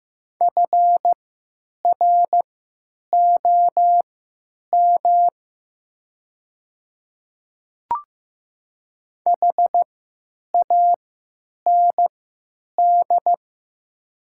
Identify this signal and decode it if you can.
Morse code, heard as a keyed tone, about 15 wpm, spaced as for 9 wpm, 700 Hz. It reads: FROM HAND
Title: 3x Character Spacing - Top 200 Words - 15wpm